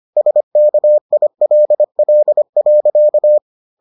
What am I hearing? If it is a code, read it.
Morse code, heard as a keyed tone, about 25 wpm, 600 Hz. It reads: SKILL.